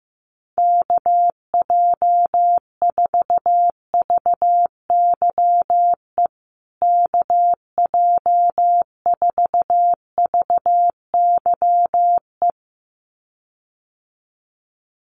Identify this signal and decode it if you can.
Morse code, heard as a keyed tone, about 15 wpm, 700 Hz. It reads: KJ4VYE KJ4VYE
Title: USA Call Signs Repeated 2x 15wpm